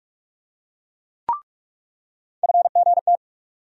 Morse code, heard as a keyed tone, about 47 wpm, 700 Hz. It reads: FGT
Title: USA Call Sign Postfixes - 3 Letters 45wpm